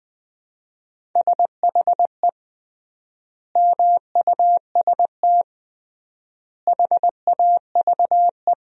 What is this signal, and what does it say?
Morse code, heard as a keyed tone, about 20 wpm, 700 Hz. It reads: SHE MUST HAVE
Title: 3x Word-Spacing - Sentences from Top 300 Words - 20wpm